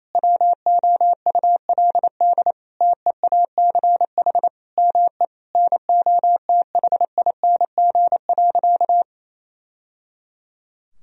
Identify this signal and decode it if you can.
Morse code, heard as a keyed tone, about 28 wpm, 700 Hz. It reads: WOULD TEACH ME NOTHING.
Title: Sentences from Top 1000 Words - Review Part 2 - 28wpm